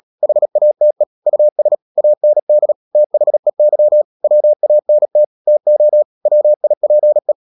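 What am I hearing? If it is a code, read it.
Morse code, heard as a keyed tone, about 37 wpm, 600 Hz. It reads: HATE US AND THEY WANT TO WIPE
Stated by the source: Morse code practice at 35–39 wpm sending English text of Ken Follett's novel Never, from CW med Gnister.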